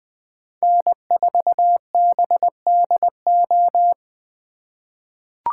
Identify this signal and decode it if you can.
Morse code, heard as a keyed tone, about 20 wpm, 700 Hz. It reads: N4BDO
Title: USA Call Signs - 5 Characters - 3x Character Spacing - 20wpm